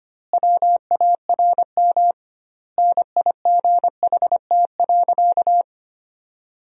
Morse code, heard as a keyed tone, about 25 wpm, 700 Hz. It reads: WARM NIGHT.
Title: 2x Word-Spacing - Sentences from Top 500 Words - 25wpm